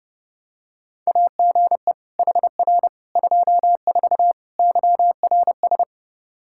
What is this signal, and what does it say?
Morse code, heard as a keyed tone, about 30 wpm, 700 Hz. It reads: AGE HR 24 YRS
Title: K1USN SST QSO Practice 30wpm